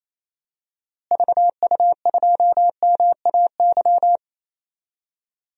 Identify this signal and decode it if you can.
Morse code, heard as a keyed tone, about 27 wpm, 700 Hz. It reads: VU2MAY
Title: DXCC-100 Call Signs - 6 Characters - Repeated 2x - 28wpm